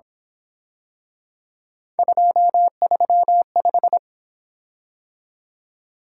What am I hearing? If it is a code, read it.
Morse code, heard as a keyed tone, about 27 wpm, 700 Hz. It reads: 235
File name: Speed-Racing - Three Letter-Number Combinations - 22wpm